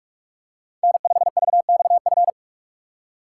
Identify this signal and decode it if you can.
Morse code, heard as a keyed tone, about 49 wpm, 700 Hz. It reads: N5VXF